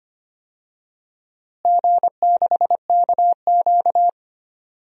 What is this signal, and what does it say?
Morse code, heard as a keyed tone, about 25 wpm, 700 Hz. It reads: G6KQ
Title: DXCC-100 Call Signs - 4 Characters - Repeated 3x - 25wpm